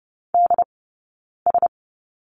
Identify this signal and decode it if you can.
Morse code, heard as a keyed tone, about 30 wpm, spaced as for 9 wpm, 690 Hz. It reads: DS
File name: Lesson 19d - Letter w - Two Characters 30wpm